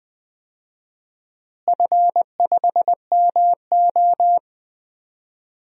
Morse code, heard as a keyed tone, about 20 wpm, 700 Hz. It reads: F5MO
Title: DXCC-100 Call Signs - 4 Characters - Repeated 2x - 20wpm